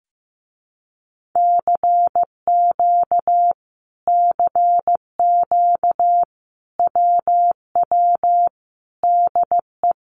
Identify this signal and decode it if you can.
Morse code, heard as a keyed tone, about 15 wpm, 700 Hz. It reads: CQ CQ WW DE
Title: Common QSO Phrases Encore 15wpm